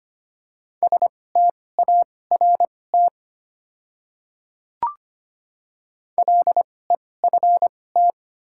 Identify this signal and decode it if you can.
Morse code, heard as a keyed tone, about 25 wpm, spaced as for 18 wpm, 700 Hz. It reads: START LEFT
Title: 2x Character Spacing - Top 300 Words - 25wpm